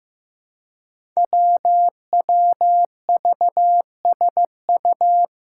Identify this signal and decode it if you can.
Morse code, heard as a keyed tone, about 15 wpm, 700 Hz. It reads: WWVSU
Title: Five Challenging Letter-Number Combinations 15wpm